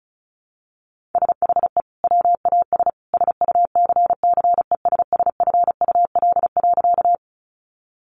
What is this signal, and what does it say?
Morse code, heard as a keyed tone, about 34 wpm, 700 Hz. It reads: SHE WAS SUCCESSFUL.
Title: Sentences from Top 900 Words 35wpm